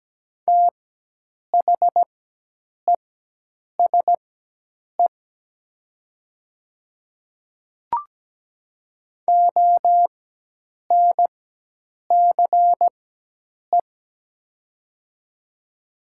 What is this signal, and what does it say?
Morse code, heard as a keyed tone, about 17 wpm, spaced as for 8 wpm, 700 Hz. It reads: THESE ONCE
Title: 4x Character Spacing - Top 300 Words - 17wpm